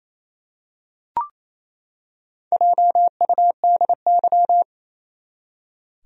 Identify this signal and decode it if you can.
Morse code, heard as a keyed tone, about 28 wpm, 700 Hz. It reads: JUDY
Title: USA Common Names - Max 5 Letters - 28wpm